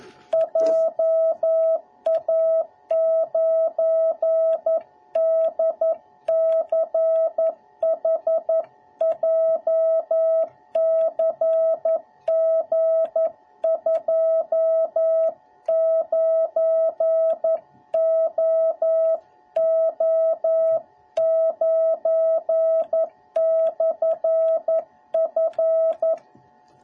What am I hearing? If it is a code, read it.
Morse code, heard as a keyed tone, about 11 wpm, 650 Hz. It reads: JA9DCH JCG29OO9/F